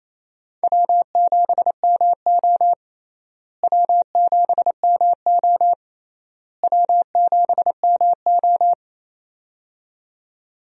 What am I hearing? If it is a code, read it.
Morse code, heard as a keyed tone, about 28 wpm, 700 Hz. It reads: W7MO W7MO W7MO
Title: USA Call Signs - 4 Characters - Repeated 3x - 28wpm